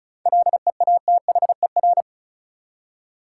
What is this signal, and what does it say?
Morse code, heard as a keyed tone, about 36 wpm, 700 Hz. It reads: LEATHER
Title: Seven Letter Words - 35wpm